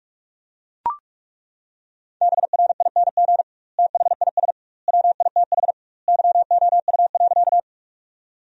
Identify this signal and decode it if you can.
Morse code, heard as a keyed tone, about 41 wpm, 700 Hz. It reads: BRING THIS WITH YOU.
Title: 2x Word-Spacing - Sentences from Top 500 Words - 45wpm